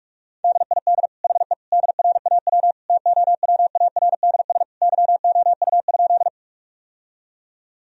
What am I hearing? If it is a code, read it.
Morse code, heard as a keyed tone, about 46 wpm, 700 Hz. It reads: DID HE DRAW TOWARDS YOU?
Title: Sentences from Top 1000 Words - Review Part 2 - 45wpm